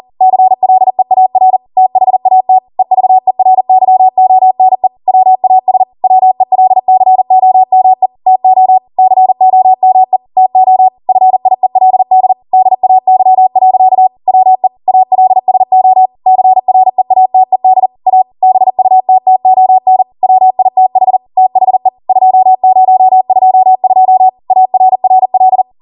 Morse code, heard as a keyed tone, about 40 wpm, 750 Hz. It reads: CLEAR THAT EVERYONE WAS WELCOME TO COME TO FIELD DAY. WE ALSO CREATED A BUTTON WITH THE 2022 ARRL